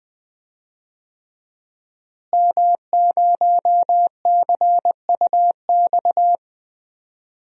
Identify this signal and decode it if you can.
Morse code, heard as a keyed tone, about 20 wpm, 700 Hz. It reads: M0CUX
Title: DXCC-100 Call Signs - 5 Characters - Repeated 3x - 20wpm